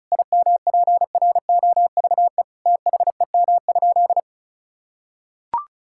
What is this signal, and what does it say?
Morse code, heard as a keyed tone, about 36 wpm, 700 Hz. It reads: IMPROVE THEM?
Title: Sentences from Top 1000 Words - Review Part 5 - 35wpm